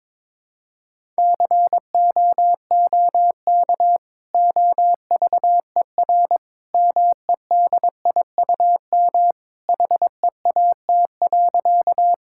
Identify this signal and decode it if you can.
Morse code, heard as a keyed tone, about 22 wpm, 700 Hz. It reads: COOK OVER MEDIUM HEAT.
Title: Sentences from Top 1000 Words - Review Part 6 - 22wpm